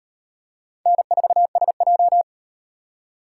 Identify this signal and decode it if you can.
Morse code, heard as a keyed tone, about 38 wpm, 700 Hz. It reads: N4SJ